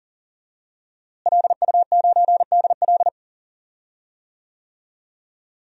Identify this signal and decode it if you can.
Morse code, heard as a keyed tone, about 40 wpm, 700 Hz. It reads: LU9DL
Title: DXCC-100 Call Signs - 5 Characters - Repeated 2x - 40wpm